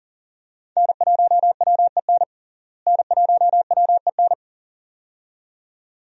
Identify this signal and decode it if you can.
Morse code, heard as a keyed tone, about 37 wpm, 700 Hz. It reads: N1WEN N1WEN
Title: USA Call Signs - 5 Characters - Repeated 2x - 40wpm